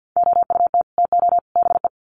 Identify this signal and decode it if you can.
Morse code, heard as a keyed tone, about 50 wpm, 700 Hz. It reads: OUT TO BE
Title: Sets of 3 Words 50wpm